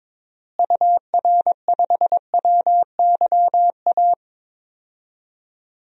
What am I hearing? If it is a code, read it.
Morse code, heard as a keyed tone, about 22 wpm, 700 Hz. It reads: UR5WYA
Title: DXCC-100 Call Signs - 6 Characters - Repeated 2x - 22wpm